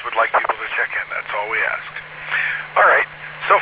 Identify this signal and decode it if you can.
Emergency Alert System (SAME) tones, heard as harsh buzzing bursts, with speech. It is heard from 0.6 to 3.4 s.